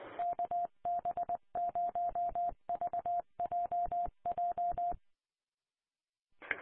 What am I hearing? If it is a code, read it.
Morse code, heard as a keyed tone, about 20 wpm, 700 Hz. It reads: KB0VJJ
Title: Colorado Connection Statewide Linked Repeaters 447.525